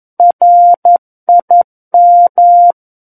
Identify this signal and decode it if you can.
Morse code, heard as a keyed tone, about 11 wpm, 700 Hz. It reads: RIM
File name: Book One / 02-Chapter 2 - 11 WPM